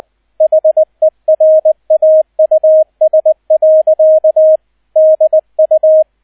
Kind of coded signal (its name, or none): Morse code